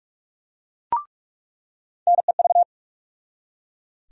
Morse code, heard as a keyed tone, about 42 wpm, 700 Hz. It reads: NEV
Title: USA Call Sign Postfixes - 3 Letters 45wpm